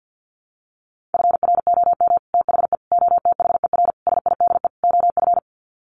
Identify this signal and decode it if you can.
Morse code, heard as a keyed tone, about 50 wpm, 700 Hz. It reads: FROM THE OTHER SIDE OF
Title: Sets of 5 Words Encore 50wpm